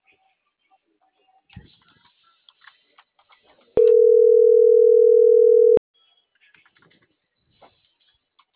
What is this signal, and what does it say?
Ringback tone (North America), 440 + 480 Hz.